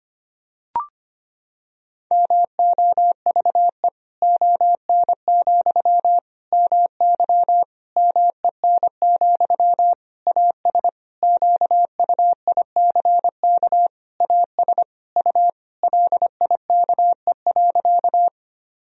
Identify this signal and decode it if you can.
Morse code, heard as a keyed tone, about 25 wpm, 700 Hz. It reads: MOVE ON, MY MEN, AS QUICK AS U LIKE.
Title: Sentences from Top 500 Words and Abbreviations 25wpm